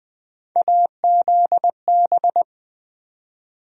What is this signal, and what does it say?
Morse code, heard as a keyed tone, about 20 wpm, 700 Hz. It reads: AZB
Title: USA Call Sign Postfixes - 3 Letters 20wpm